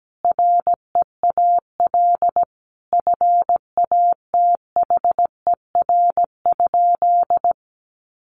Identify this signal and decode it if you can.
Morse code, heard as a keyed tone, about 17 wpm, 700 Hz. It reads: REAL FATHER?
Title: Sentences from Top 300 Words 17wpm